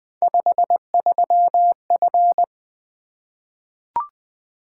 Morse code, heard as a keyed tone, about 20 wpm, 700 Hz. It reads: 53F
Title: Three Letter-Number Combinations - 20wpm